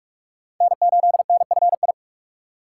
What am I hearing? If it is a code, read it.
Morse code, heard as a keyed tone, about 46 wpm, 700 Hz. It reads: N8NFI
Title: USA Call Signs - 3x Character Spacing - 45wpm